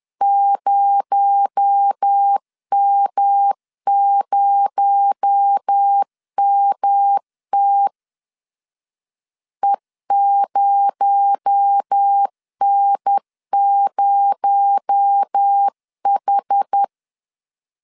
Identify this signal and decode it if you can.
Morse code, heard as a keyed tone, about 10 wpm, 800 Hz. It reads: 0M0MT E0N0H
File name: LEZIONI / 6 PKX0 / 6 PKX0 10wpm 2707